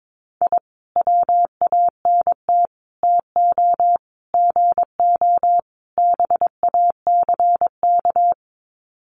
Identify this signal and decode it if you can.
Morse code, heard as a keyed tone, about 22 wpm, 700 Hz. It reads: I WANT TO GO BACK
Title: Sets of 5 Words Encore 22wpm